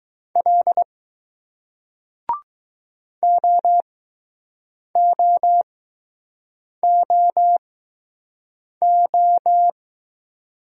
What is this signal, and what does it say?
Morse code, heard as a keyed tone, about 20 wpm, spaced as for 6 wpm, 700 Hz. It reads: L OOOO